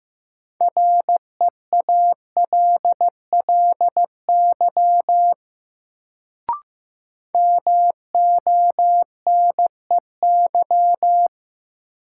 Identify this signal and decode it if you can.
Morse code, heard as a keyed tone, about 15 wpm, 700 Hz. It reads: REALLY MONEY
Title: Sets of 4 Words - New Word Drill - 15wpm